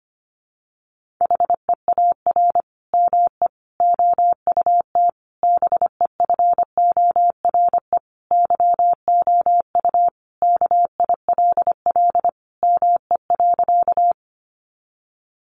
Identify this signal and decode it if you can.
Morse code, heard as a keyed tone, about 25 wpm, 700 Hz. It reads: HEAR ME OUT BEFORE YOU KILL ME.
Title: Sentences from Top 1400 Words 25wpm